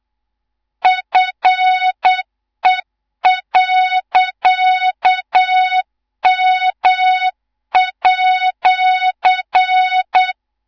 Morse code, heard as a keyed tone, about 8 wpm, 3700 Hz, with noise. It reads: FE.M@
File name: Morze-HA2MN / hang / 18 IRASJELEK2